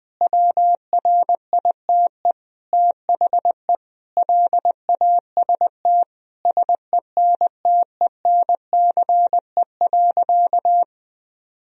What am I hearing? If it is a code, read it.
Morse code, heard as a keyed tone, about 20 wpm, 700 Hz. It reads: WRITE THE LAST SENTENCE.
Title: Sentences from Top 1000 Words - Review Part 1 - 20wpm